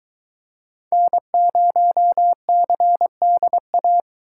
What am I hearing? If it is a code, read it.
Morse code, heard as a keyed tone, about 22 wpm, 700 Hz. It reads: N0CDA